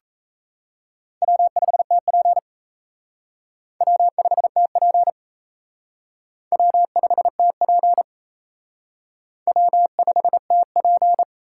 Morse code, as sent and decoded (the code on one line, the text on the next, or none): .-- ..... - .--. / .-- ..... - .--. / .-- ..... - .--. / .-- ..... - .--.
W5TP W5TP W5TP W5TP